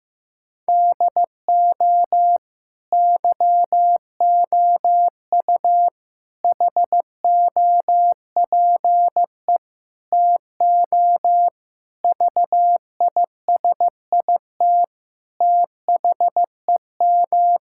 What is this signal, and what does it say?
Morse code, heard as a keyed tone, about 15 wpm, 700 Hz. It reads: DO YOU HOPE TO VISIT THEM
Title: Sentences from Top 1000 Words - Review Part 6 - 15wpm